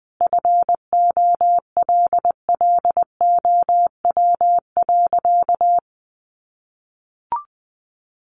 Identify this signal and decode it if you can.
Morse code, heard as a keyed tone, about 20 wpm, 700 Hz. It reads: FOLLOW.
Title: Sentences from Top 1500 Words 20wpm